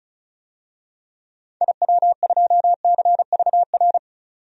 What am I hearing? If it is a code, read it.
Morse code, heard as a keyed tone, about 36 wpm, 700 Hz. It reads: IW2CVR
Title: DXCC-100 Call Signs - 6 Characters - 35wpm